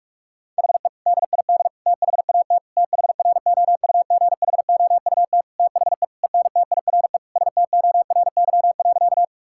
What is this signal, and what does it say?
Morse code, heard as a keyed tone, about 46 wpm, 700 Hz. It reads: HE DID THAT THROUGHOUT THE ENTIRE STORY.
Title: Sentences from Top 1000 Words - Review Part 5 - 45wpm